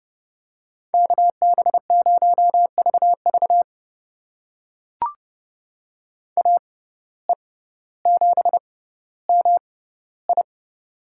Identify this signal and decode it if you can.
Morse code, heard as a keyed tone, about 30 wpm, spaced as for 15 wpm, 700 Hz. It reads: KB0VV AE7MI
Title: USA Call Signs - 5 Characters - 6x Character Spacing - 30wpm